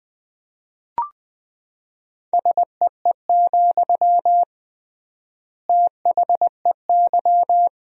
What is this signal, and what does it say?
Morse code, heard as a keyed tone, about 21 wpm, 700 Hz. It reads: SEE, THEY
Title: 3x Word-Spacing - Sentences from Top 200 Words - 20wpm